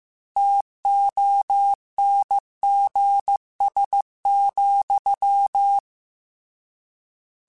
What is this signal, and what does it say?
Morse code, heard as a keyed tone, about 15 wpm, 800 Hz. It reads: TONGS,
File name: Misc / Misc-mp3s / Word Pairs @13